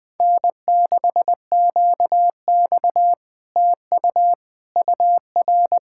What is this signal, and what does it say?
Morse code, heard as a keyed tone, about 20 wpm, 700 Hz. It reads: N6QX TU UR